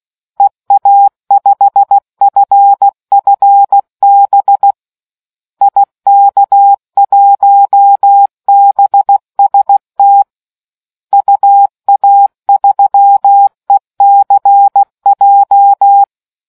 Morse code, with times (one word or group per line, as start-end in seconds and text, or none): 0.4-4.7 EA5FFB
5.6-10.2 IK1BST
11.1-16.0 UA3ECJ